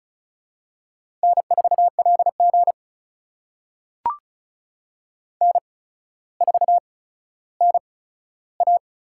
Morse code, heard as a keyed tone, about 36 wpm, spaced as for 15 wpm, 700 Hz. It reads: N4LG N4NA
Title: USA Call Signs - 4 Characters - 8x Character Spacing - 35wpm